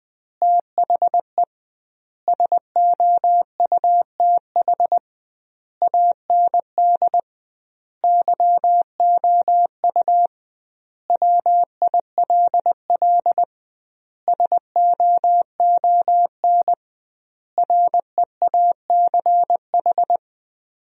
Morse code, heard as a keyed tone, about 20 wpm, 700 Hz. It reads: THE SOUTH AND YOU WILL SOON REACH
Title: 2x Word-Spacing - Sentences from Top 400 Words - 20wpm